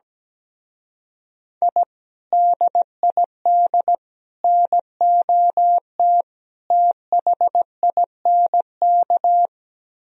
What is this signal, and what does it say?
Morse code, heard as a keyed tone, about 17 wpm, 700 Hz. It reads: I DID NOT THINK
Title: Speed-Racing - Sets of 4 Words 17wpm